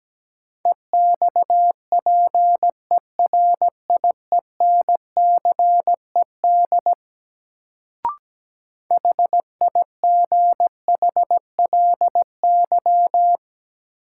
Morse code, heard as a keyed tone, about 17 wpm, 700 Hz. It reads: EXPERIENCED HIGHLY
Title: Sentences from Top 1100 Words - New Word Drill Alt - 17wpm